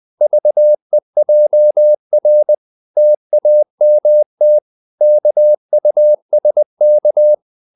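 Morse code, heard as a keyed tone, about 20 wpm, 600 Hz. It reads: VEJR TAMT KUSK
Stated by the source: Morse code practice of short Danish words at 20 wpm, from CW med Gnister.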